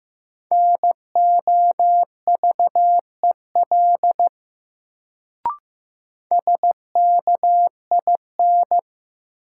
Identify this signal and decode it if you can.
Morse code, heard as a keyed tone, about 15 wpm, 700 Hz. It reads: NOVEL SKIN
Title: Sentences from Top 1300 Words - New Word Drill - 15wpm